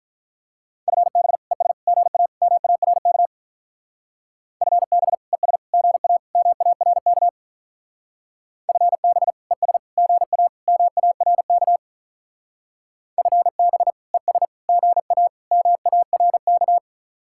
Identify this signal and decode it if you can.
Morse code, heard as a keyed tone, about 43 wpm, 700 Hz. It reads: FB ES GA MARK FB ES GA MARK FB ES GA MARK FB ES GA MARK
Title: Speed-Racing Common QSO Phrases Encore - 35wpm